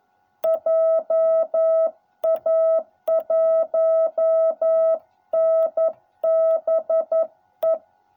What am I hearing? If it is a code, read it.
Morse code, heard as a keyed tone, about 11 wpm, 660 Hz. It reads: JA1NBE